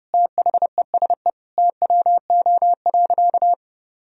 Morse code, heard as a keyed tone, about 30 wpm, 700 Hz. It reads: THESE TWO.
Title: Sentences from Top 1000 Words - Review Part 1 - 30wpm